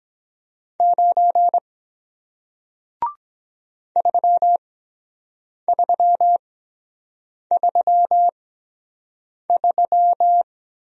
Morse code, heard as a keyed tone, about 23 wpm, spaced as for 7 wpm, 700 Hz. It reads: 9 3333